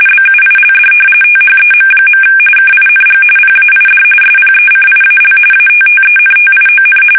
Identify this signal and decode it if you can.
RTTY, about 45.45 baud. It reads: BUILD THEM. KING PAID SPECIAL ATTENTION
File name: fsk850